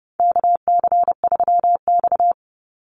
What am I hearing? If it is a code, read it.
Morse code, heard as a keyed tone, about 30 wpm, 700 Hz. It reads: KC3X
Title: Call Signs Repeated 3x - 30wpm